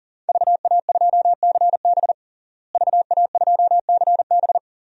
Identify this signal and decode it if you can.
Morse code, heard as a keyed tone, about 40 wpm, 700 Hz. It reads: VA2CB VA2CB
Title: DXCC-100 Call Signs - 5 Characters - Repeated 2x - 40wpm